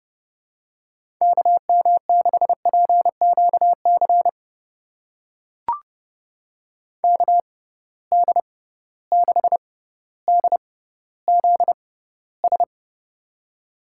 Morse code, heard as a keyed tone, about 30 wpm, spaced as for 15 wpm, 700 Hz. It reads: KM6PQC KD6DZS
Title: USA Call Signs - 6 Characters - 6x Character Spacing - 30wpm